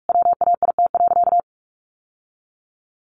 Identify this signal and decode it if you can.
Morse code, heard as a keyed tone, about 44 wpm, 700 Hz. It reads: WAIT.
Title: Sentences from Top 800 Words 45wpm